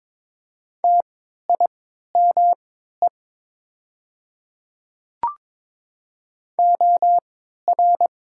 Morse code, heard as a keyed tone, about 22 wpm, spaced as for 12 wpm, 700 Hz. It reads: TIME OR
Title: 3x Character Spacing - Top 100 Words - 22wpm